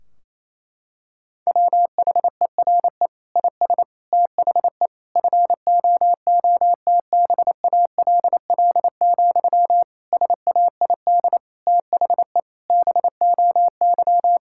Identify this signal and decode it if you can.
Morse code, heard as a keyed tone, about 28 wpm, 700 Hz. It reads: WHERE IS THE FOOTBALL, SAID THE BOY
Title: Sentences from Top 1600 Words 28wpm